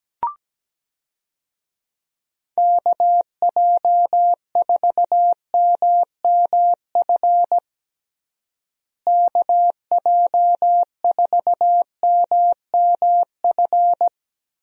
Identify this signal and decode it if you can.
Morse code, heard as a keyed tone, about 17 wpm, 700 Hz. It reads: KJ4MMF KJ4MMF
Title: USA Call Signs - 6 Characters - Repeated 3x - 17wpm